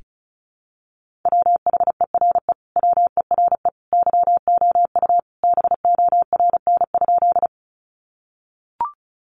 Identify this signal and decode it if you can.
Morse code, heard as a keyed tone, about 34 wpm, 700 Hz. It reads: WHERE WERE YOU BORN?
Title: Sentences from Top 1000 Words 35wpm